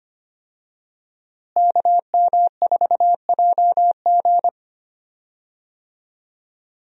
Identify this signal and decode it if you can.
Morse code, heard as a keyed tone, about 25 wpm, 700 Hz. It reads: KM4JG
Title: USA Call Signs - 5 Characters - Repeated 2x - 25wpm